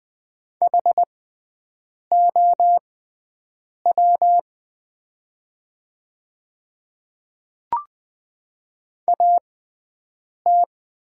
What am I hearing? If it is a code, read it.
Morse code, heard as a keyed tone, about 20 wpm, spaced as for 7 wpm, 700 Hz. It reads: HOW AT
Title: 6x Character Spacing - Top 100 Words - 20wpm